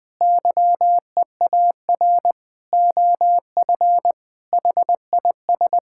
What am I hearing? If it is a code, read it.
Morse code, heard as a keyed tone, about 21 wpm, 700 Hz. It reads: YEAR OF HIS
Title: Sentences with 8-Word Phrases - 20wpm